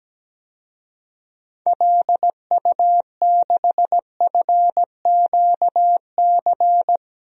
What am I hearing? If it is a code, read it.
Morse code, heard as a keyed tone, about 17 wpm, 700 Hz. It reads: LU6FQC
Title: DXCC-100 Call Signs - 6 Characters - 17wpm